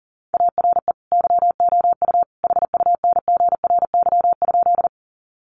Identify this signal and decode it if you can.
Morse code, heard as a keyed tone, about 40 wpm, 700 Hz. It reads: ARE YOU HUNGRY?
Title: Sentences from Top 1400 Words 40wpm